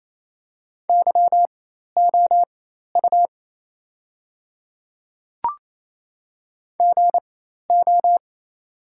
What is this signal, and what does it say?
Morse code, heard as a keyed tone, about 28 wpm, spaced as for 13 wpm, 700 Hz. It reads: YOU GO
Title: 4x Character Spacing - Top 200 Words - 28wpm